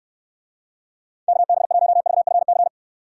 Morse code, heard as a keyed tone, about 69 wpm, 700 Hz. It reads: KB1VFC